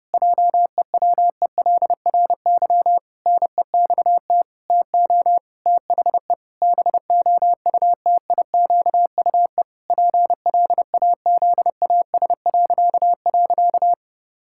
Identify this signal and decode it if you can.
Morse code, heard as a keyed tone, about 30 wpm, 700 Hz. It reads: JEWELRY NEXT TO THE BOUTIQUE PLAZAS..
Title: Pangram Challenge v2 - 30wpm